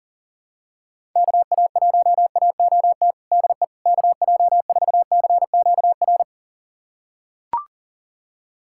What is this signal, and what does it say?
Morse code, heard as a keyed tone, about 40 wpm, 700 Hz. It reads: KA1AOT DE KJ4CQR
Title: USA Callsign DE Callsign 40wpm